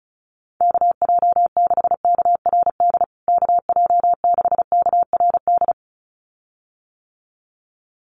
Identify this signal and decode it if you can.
Morse code, heard as a keyed tone, about 34 wpm, 700 Hz. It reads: KJ6KRD KJ6KRD
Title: USA Call Signs Repeated 2x 35wpm